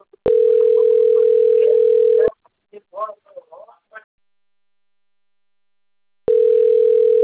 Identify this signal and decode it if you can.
Ringback tone (North America), 440 + 480 Hz, with speech.
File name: rg-1600-unknown- - - .77930